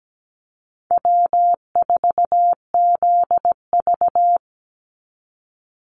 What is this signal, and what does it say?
Morse code, heard as a keyed tone, about 17 wpm, 700 Hz. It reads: W4ZV
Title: NA SOTA Chaser Callsigns 17wpm